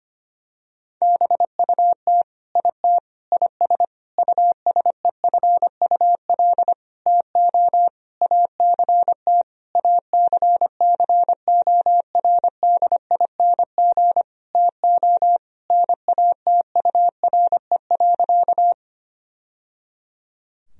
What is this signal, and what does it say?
Morse code, heard as a keyed tone, about 25 wpm, 700 Hz. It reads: BUT IT IS USEFUL TO ACT ACCORDING TO NATURE.
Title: Sentences from Top 1000 Words - Review Part 4 - 25wpm